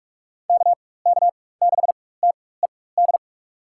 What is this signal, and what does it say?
Morse code, heard as a keyed tone, about 46 wpm, spaced as for 21 wpm, 700 Hz. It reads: KK6TED
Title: USA Call Signs - 6 Characters - 4x Character Spacing - 45wpm